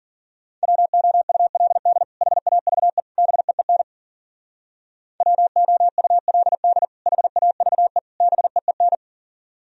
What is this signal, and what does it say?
Morse code, heard as a keyed tone, about 52 wpm, 700 Hz. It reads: WOULD HAVE BEEN WOULD HAVE BEEN